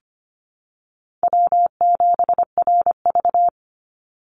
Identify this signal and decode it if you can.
Morse code, heard as a keyed tone, about 25 wpm, 700 Hz. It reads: W7RV
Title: NA SOTA Chaser Callsigns 25wpm